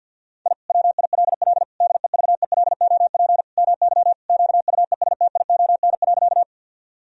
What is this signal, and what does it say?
Morse code, heard as a keyed tone, about 50 wpm, 700 Hz. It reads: I WILL DEVELOP MY QUESTION.
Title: Sentences from Top 1000 Words - Review Part 2 - 50wpm